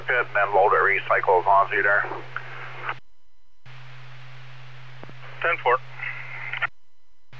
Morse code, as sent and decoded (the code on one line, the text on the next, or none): none